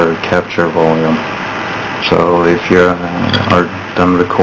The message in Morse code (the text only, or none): none